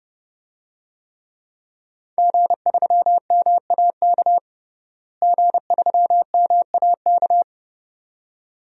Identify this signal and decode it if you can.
Morse code, heard as a keyed tone, about 30 wpm, 700 Hz. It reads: G3MAK G3MAK
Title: DXCC-100 Call Signs - 5 Characters - Repeated 2x - 30wpm